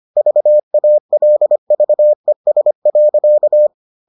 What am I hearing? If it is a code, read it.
Morse code, heard as a keyed tone, about 25 wpm, 600 Hz. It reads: VALVES.